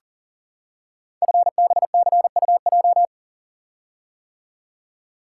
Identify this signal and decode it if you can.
Morse code, heard as a keyed tone, about 39 wpm, 690 Hz. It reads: FBCUJ
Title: Code Groups - 5 Letters - 40wpm